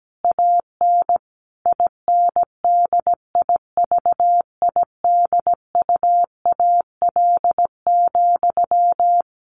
Morse code, heard as a keyed tone, about 17 wpm, 700 Hz. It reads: AN INDIVIDUAL,
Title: Sentences from Top 1400 Words 17wpm